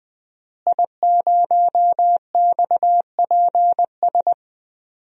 Morse code, heard as a keyed tone, about 20 wpm, 700 Hz. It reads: I0XPS
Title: DXCC-100 Call Signs - 5 Characters - Repeated 2x - 20wpm